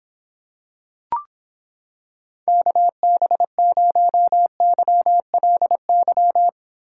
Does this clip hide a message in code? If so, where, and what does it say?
Yes: Morse code from 2.5 to 6.5 s, heard as a keyed tone, about 26 wpm, 700 Hz. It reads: KB0YLY